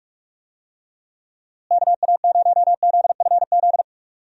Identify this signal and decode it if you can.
Morse code, heard as a keyed tone, about 41 wpm, 700 Hz. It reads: KA0ZFZ
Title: USA Call Signs - 6 Characters - Repeated 2x - 45wpm